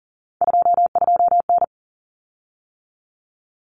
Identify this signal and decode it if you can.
Morse code, heard as a keyed tone, about 40 wpm, 700 Hz. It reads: 22N
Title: Common Cut Numbers - 3-Digit 40wpm